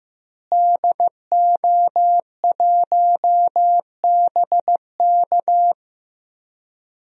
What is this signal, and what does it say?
Morse code, heard as a keyed tone, about 15 wpm, 700 Hz. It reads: DO1BK
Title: DXCC-100 Call Signs - 5 Characters - Repeated 3x - 15wpm